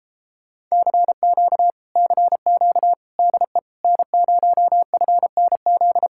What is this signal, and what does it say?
Morse code, heard as a keyed tone, about 33 wpm, 700 Hz. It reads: CQ CQ DE N0FNZ